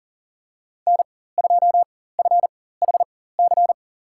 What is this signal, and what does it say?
Morse code, heard as a keyed tone, about 40 wpm, spaced as for 19 wpm, 700 Hz. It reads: N2FHC
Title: USA Call Signs - 4x Character Spacing - 40wpm